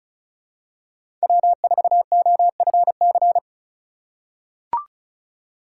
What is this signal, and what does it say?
Morse code, heard as a keyed tone, about 36 wpm, 690 Hz. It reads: W4OFC
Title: USA Call Signs - 5 Characters - 3x Character Spacing - 35wpm